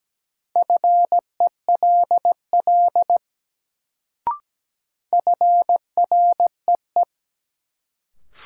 Morse code, heard as a keyed tone, about 17 wpm, 700 Hz. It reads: FELL FREE
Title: Sets of 5 Words Encore - New Word Drill - 17wpm